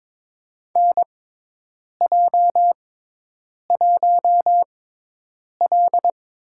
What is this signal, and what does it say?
Morse code, heard as a keyed tone, about 22 wpm, spaced as for 8 wpm, 700 Hz. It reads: NJ1L
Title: USA Call Signs - 4 Characters - 6x Character Spacing - 22wpm